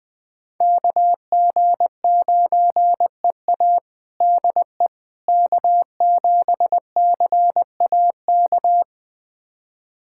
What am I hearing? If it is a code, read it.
Morse code, heard as a keyed tone, about 20 wpm, 700 Hz. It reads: KG9EA DE K7CAK
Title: USA Callsign DE Callsign - 5 Characters - 20wpm